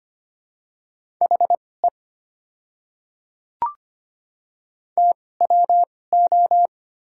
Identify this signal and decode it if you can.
Morse code, heard as a keyed tone, about 25 wpm, spaced as for 18 wpm, 690 Hz. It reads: HE TWO
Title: 2x Character Spacing - Top 100 Words - 25wpm